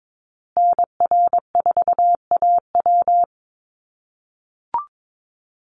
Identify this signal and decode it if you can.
Morse code, heard as a keyed tone, about 22 wpm, 700 Hz. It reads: NR4AW
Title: USA Call Signs Repeated 3x 22wpm